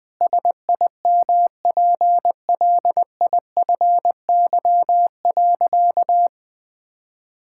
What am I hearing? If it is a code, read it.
Morse code, heard as a keyed tone, about 20 wpm, 700 Hz. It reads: SIMPLIFY.
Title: Sentences with 7-Word Phrases - 20wpm